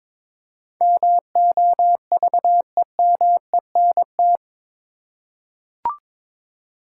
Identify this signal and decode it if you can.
Morse code, heard as a keyed tone, about 22 wpm, 700 Hz. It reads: MOVEMENT
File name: practice / 22wpm / Eight Letter Words - 22wpm